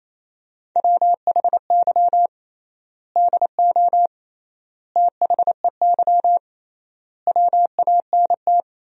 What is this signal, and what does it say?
Morse code, heard as a keyed tone, about 28 wpm, 700 Hz. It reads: WHY DO THEY WANT
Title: 3x Word-Spacing - Sentences from Top 400 Words - 28wpm